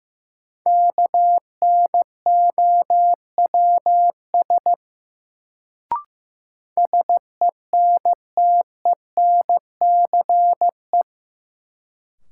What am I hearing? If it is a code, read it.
Morse code, heard as a keyed tone, about 15 wpm, 700 Hz. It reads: KNOWS SENTENCE
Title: Sentences from Top 1000 Words - Review Part 1 - New Word Drill - 15wpm